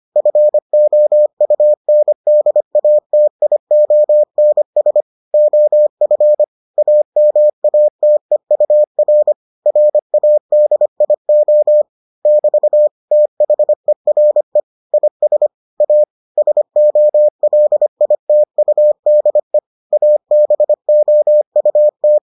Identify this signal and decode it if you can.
Morse code, heard as a keyed tone, about 25 wpm, 600 Hz. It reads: FOUNDATIONS OF AMATEUR RADIO <BT> THERE IS A SOLITUDE ABOUT